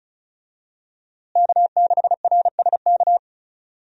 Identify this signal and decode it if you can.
Morse code, heard as a keyed tone, about 36 wpm, 700 Hz. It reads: K6RSK
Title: USA Call Signs - 5 Characters - Repeated 2x - 35wpm